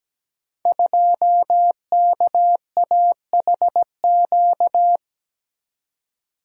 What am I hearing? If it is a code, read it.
Morse code, heard as a keyed tone, about 17 wpm, 700 Hz. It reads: 2KAHQ
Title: Code Groups - 5 Letter:Numbers - 17wpm